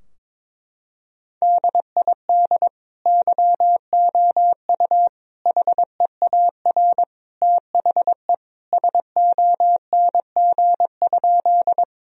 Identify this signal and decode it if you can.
Morse code, heard as a keyed tone, about 22 wpm, 700 Hz. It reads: DID YOU HEAR THE SONG?
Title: Sentences from Top 1000 Words - Review Part 2 - 22wpm